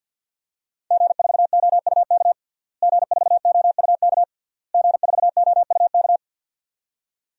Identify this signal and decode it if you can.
Morse code, heard as a keyed tone, about 51 wpm, 700 Hz. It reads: G4OUK G4OUK G4OUK
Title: DXCC-100 Call Signs - 5 Characters - Repeated 3x - 50wpm